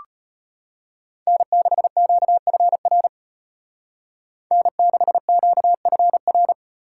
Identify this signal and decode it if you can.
Morse code, heard as a keyed tone, about 30 wpm, 700 Hz. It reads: N6QFR N6QFR
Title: Speed Racing - USA 1x3 Call Signs 25wpm